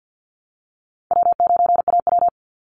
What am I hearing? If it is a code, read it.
Morse code, heard as a keyed tone, about 50 wpm, 700 Hz. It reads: W9AW
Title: Easy Call Signs - 50wpm